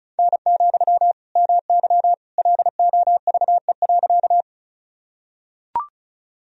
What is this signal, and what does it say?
Morse code, heard as a keyed tone, about 36 wpm, 700 Hz. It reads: N, MY LOVE.
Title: Sentences from Top 400 Words and Abbreviations 35wpm